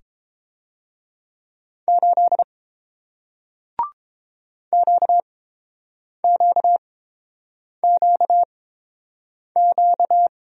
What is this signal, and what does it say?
Morse code, heard as a keyed tone, about 30 wpm, spaced as for 7 wpm, 700 Hz. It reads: 8 QQQQ